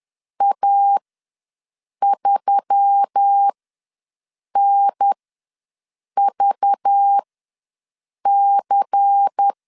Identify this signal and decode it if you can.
Morse code, heard as a keyed tone, about 10 wpm, spaced as for 6 wpm, 800 Hz. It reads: A3NVC
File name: LEZIONI / 6 PKX0 / 6 PK ws 10wpm 3745